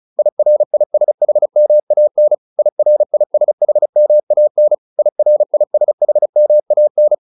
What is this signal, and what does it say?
Morse code, heard as a keyed tone, about 35 wpm, 600 Hz. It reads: IRISHMAN IRISHMAN IRISHMAN
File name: IR8 35